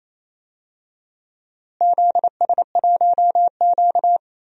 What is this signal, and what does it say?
Morse code, heard as a keyed tone, about 28 wpm, 700 Hz. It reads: ZS1Q
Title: DXCC-100 Call Signs - 4 Characters - Repeated 3x - 28wpm